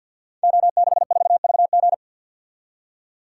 Morse code, heard as a keyed tone, about 52 wpm, 700 Hz. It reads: O64VG
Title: Code Groups - 5 Letter:Numbers - 50wpm